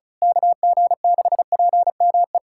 Morse code, heard as a keyed tone, about 34 wpm, 700 Hz. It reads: KG6PME
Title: USA Callsign DE Callsign 35wpm